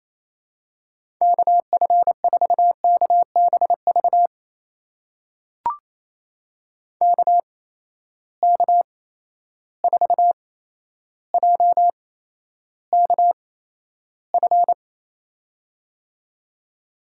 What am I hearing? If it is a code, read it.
Morse code, heard as a keyed tone, about 27 wpm, spaced as for 12 wpm, 700 Hz. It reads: KF4KBV KK4JKF
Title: USA Call Signs - 6 Characters - 8x Character Spacing - 28wpm